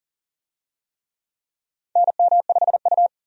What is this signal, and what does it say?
Morse code, heard as a keyed tone, about 40 wpm, 700 Hz. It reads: NM5U
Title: USA Call Signs - 4 Characters - 40wpm